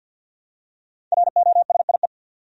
Morse code, heard as a keyed tone, about 48 wpm, 700 Hz. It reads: ROSIE
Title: USA Common Names - Max 5 Letters - 50wpm